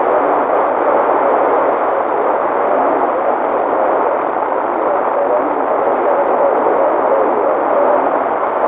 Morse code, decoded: none